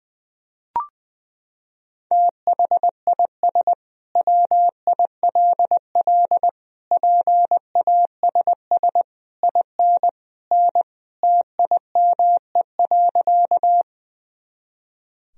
Morse code, heard as a keyed tone, about 20 wpm, 700 Hz. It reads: THIS WILL PASS IN N TIME.
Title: Sentences from Top 400 Words and Abbreviations 20wpm